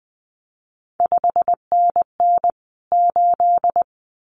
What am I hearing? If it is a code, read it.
Morse code, heard as a keyed tone, about 20 wpm, 700 Hz. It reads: 5NN 8
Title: CQ WW DX Contest Exchange - 20wpm